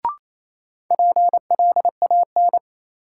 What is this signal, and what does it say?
Morse code, heard as a keyed tone, about 28 wpm, 700 Hz. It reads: PLAN
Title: Sentences from Top 500 Words - New Word Drill - 28wpm